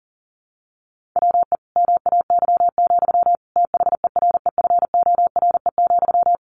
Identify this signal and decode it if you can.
Morse code, heard as a keyed tone, about 40 wpm, 700 Hz. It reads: WE MAY, THEREFORE,
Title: Sentences from Top 1500 Words 40wpm